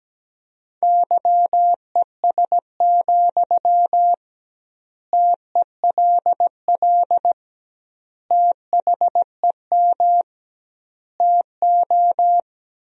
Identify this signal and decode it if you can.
Morse code, heard as a keyed tone, about 17 wpm, 700 Hz. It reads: YES, TELL THEM TO
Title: 2x Word-Spacing - Sentences from Top 500 Words - 17wpm